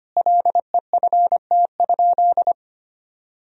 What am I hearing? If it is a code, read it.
Morse code, heard as a keyed tone, about 25 wpm, 700 Hz. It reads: LEFT?
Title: Sentences from Top 1000 Words - Review Part 2 - 25wpm